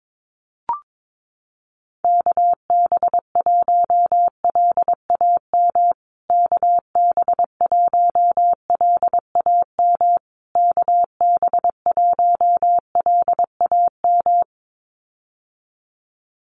Morse code, heard as a keyed tone, about 22 wpm, 700 Hz. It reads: KB1LAM KB1LAM KB1LAM
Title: USA Call Signs Repeated 3x 22wpm